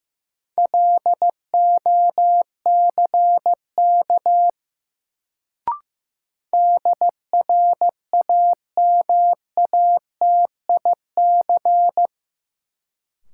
Morse code, heard as a keyed tone, about 15 wpm, 700 Hz. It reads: LOCK DRAMATIC
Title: Sentences from Top 1300 Words - New Word Drill - 15wpm